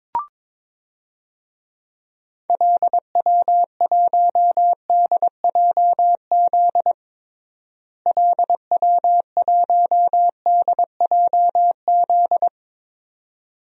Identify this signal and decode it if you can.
Morse code, heard as a keyed tone, about 22 wpm, 700 Hz. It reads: LW1DJZ LW1DJZ
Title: DXCC-100 Call Signs - 6 Characters - Repeated 2x - 22wpm